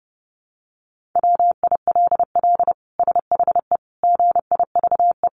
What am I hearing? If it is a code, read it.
Morse code, heard as a keyed tone, about 28 wpm, 700 Hz. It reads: WILL SHE GIVE
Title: Sentences from Top 1400 Words 30wpm